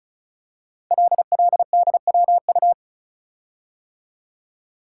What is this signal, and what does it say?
Morse code, heard as a keyed tone, about 36 wpm, 700 Hz. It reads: LLDWU
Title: Code Groups - 5 Letter:Numbers - 35wpm